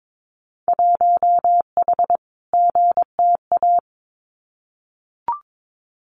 Morse code, heard as a keyed tone, about 22 wpm, 700 Hz. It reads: 1H GTA
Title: Winter Field Day Exchanges Encore 22wpm